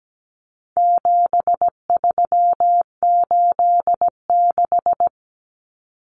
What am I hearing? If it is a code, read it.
Morse code, heard as a keyed tone, about 17 wpm, 700 Hz. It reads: 7386
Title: Common Cut Numbers - 4-Digit 17wpm